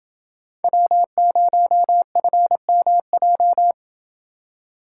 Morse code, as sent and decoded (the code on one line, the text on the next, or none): .-- ----- ..-. -- .---
W0FMJ